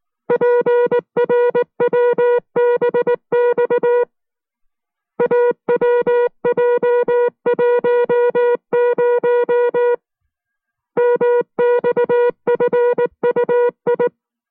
Morse code, heard as a keyed tone, about 19 wpm, 480 Hz. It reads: PRWBX AWJ10 MXFUI